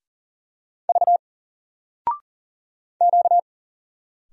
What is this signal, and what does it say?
Morse code, heard as a keyed tone, about 40 wpm, 700 Hz. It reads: V Q
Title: Single Letter-Number 40wpm